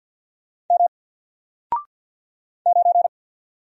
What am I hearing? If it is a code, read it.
Morse code, heard as a keyed tone, about 48 wpm, 700 Hz. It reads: M 9
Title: Single Letter-Number Alt 50wpm